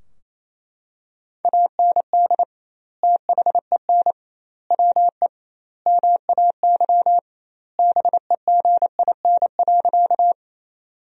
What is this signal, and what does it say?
Morse code, heard as a keyed tone, about 28 wpm, 700 Hz. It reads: AND THEN WE MAY BEGIN.
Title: 2x Word-Spacing - Sentences from Top 300 Words - 28wpm